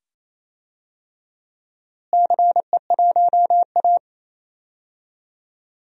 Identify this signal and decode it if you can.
Morse code, heard as a keyed tone, about 27 wpm, 700 Hz. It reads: CE1A
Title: DXCC-100 Call Signs - 4 Characters - Repeated 2x - 28wpm